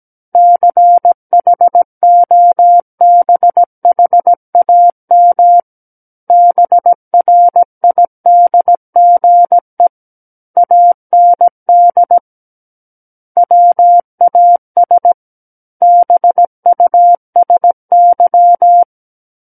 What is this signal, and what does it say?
Morse code, heard as a keyed tone, about 17 wpm, 700 Hz. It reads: CHOBHAM BRIDGE AND WAS BUSY
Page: SKCC Learning Center - Morse Code Practice Files